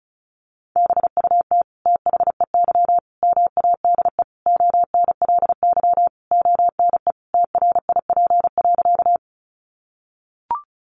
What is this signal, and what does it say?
Morse code, heard as a keyed tone, about 34 wpm, 700 Hz. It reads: BUT THEY MADE ONLY ONE TRIP.
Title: Sentences from Top 1300 Words 35wpm